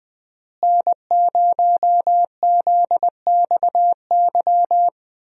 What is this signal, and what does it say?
Morse code, heard as a keyed tone, about 20 wpm, 700 Hz. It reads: N0ZXY
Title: USA Call Signs - 5 Characters - Repeated 3x - 20wpm